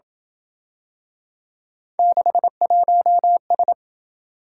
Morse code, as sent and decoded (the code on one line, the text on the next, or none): -.... .---- ...
61S